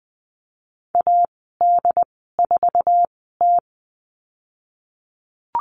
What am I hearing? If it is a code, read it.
Morse code, heard as a keyed tone, about 20 wpm, spaced as for 15 wpm, 700 Hz. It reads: AD4T
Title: 088 Lesson 15g - Number 5 - Call Signs 20wpm